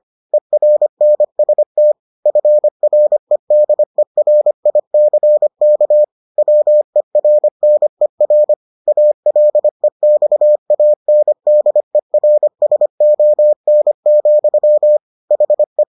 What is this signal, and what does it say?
Morse code, heard as a keyed tone, about 25 wpm, 600 Hz. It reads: ERNST FREDERICK WERNER ALEXANDERSON, HE